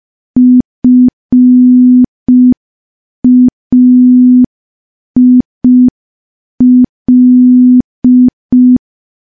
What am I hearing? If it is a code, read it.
Morse code, heard as a keyed tone, about 5 wpm, 250 Hz. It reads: FAIL
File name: morse / morse-training-05WPM